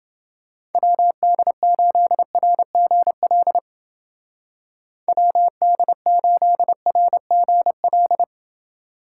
Morse code, heard as a keyed tone, about 31 wpm, 700 Hz. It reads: WD8RGL WD8RGL